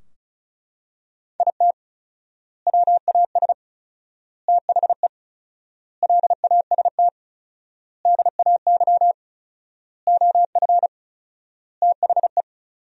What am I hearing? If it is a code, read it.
Morse code, heard as a keyed tone, about 34 wpm, 700 Hz. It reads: IT WAS THE LAST DAY OF THE
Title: 4x Word-Spacing - Sentences from Top 300 Words - 35wpm